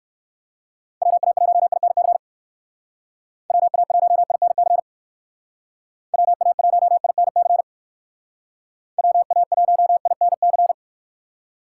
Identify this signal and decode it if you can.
Morse code, heard as a keyed tone, about 58 wpm, 700 Hz. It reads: WA1INC WA1INC WA1INC WA1INC